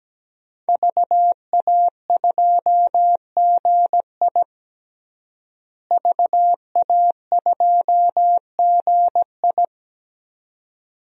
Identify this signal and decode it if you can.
Morse code, heard as a keyed tone, about 17 wpm, 700 Hz. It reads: VA2GI VA2GI
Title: DXCC-100 Call Signs - 5 Characters - Repeated 2x - 17wpm